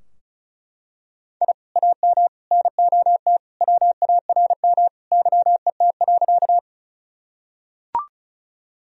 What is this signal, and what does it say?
Morse code, heard as a keyed tone, about 36 wpm, 700 Hz. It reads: I AM NOT WARM YET.
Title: Sentences from Top 1000 Words - Review Part 2 - 35wpm